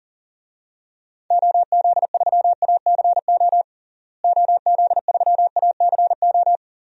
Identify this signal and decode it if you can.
Morse code, heard as a keyed tone, about 40 wpm, 700 Hz. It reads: OZ3ACO OZ3ACO
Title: DXCC-100 Call Signs - 6 Characters - Repeated 3x - 40wpm